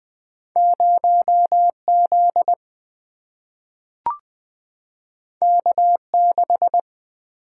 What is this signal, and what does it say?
Morse code, heard as a keyed tone, about 20 wpm, 700 Hz. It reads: 0Z K6
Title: Single-Letter Single-Number 20wpm